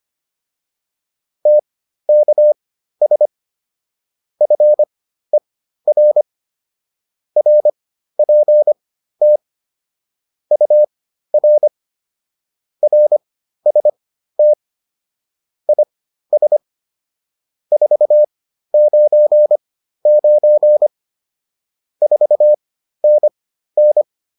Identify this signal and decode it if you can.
Morse code, heard as a keyed tone, about 25 wpm, spaced as for 13 wpm, 600 Hz. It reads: TKS FER RPT UR RST IS 499 4NN
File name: QSO 305 13